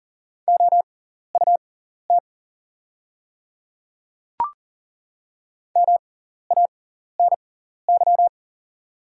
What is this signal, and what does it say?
Morse code, heard as a keyed tone, about 40 wpm, spaced as for 14 wpm, 700 Hz. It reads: OUT MANY
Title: 6x Character Spacing - Top 100 Words - 40wpm